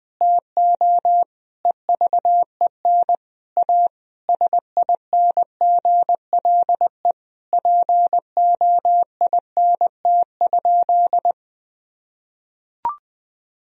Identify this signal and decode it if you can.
Morse code, heard as a keyed tone, about 20 wpm, 700 Hz. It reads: TO EVEN A SINGLE POINT?
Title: Sentences from Top 1000 Words - Review Part 4 - 20wpm